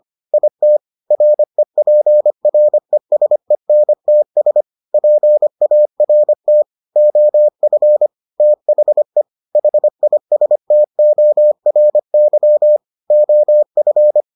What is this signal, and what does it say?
Morse code, heard as a keyed tone, about 25 wpm, 600 Hz. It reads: IT REPRESENTS PART OF THE HISTORY OF